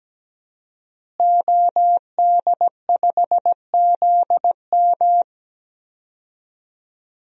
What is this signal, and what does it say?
Morse code, heard as a keyed tone, about 17 wpm, 700 Hz. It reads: OD5ZM
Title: DXCC-100 Call Signs - 5 Characters - Repeated 3x - 17wpm